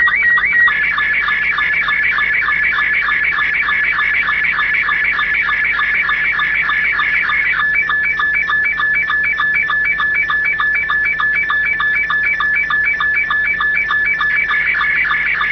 SSTV (slow-scan television), Robot 72.